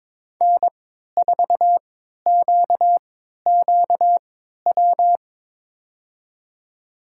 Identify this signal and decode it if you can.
Morse code, heard as a keyed tone, about 22 wpm, spaced as for 12 wpm, 700 Hz. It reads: N4QQW
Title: USA Call Signs - 5 Characters - 3x Character Spacing - 22wpm